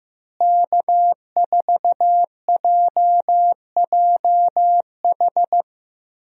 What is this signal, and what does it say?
Morse code, heard as a keyed tone, about 15 wpm, 700 Hz. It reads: K4JJH
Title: USA Call Signs - 5 Characters - Repeated 2x - 15wpm